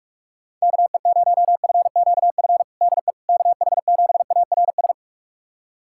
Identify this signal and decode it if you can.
Morse code, heard as a keyed tone, about 46 wpm, 690 Hz. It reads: KE0FQF DE KH7ARS
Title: USA Callsign DE Callsign - 6 Characters - 45wpm